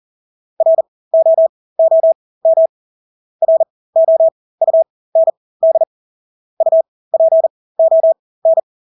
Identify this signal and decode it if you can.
Morse code, heard as a keyed tone, about 40 wpm, spaced as for 20 wpm, 650 Hz. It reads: ROOM ROUND UPON
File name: Engelske ord 40-20 wpm